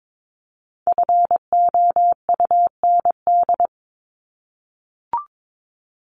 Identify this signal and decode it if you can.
Morse code, heard as a keyed tone, about 22 wpm, 700 Hz. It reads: FOUND
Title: Top 300 Words Repeated 22wpm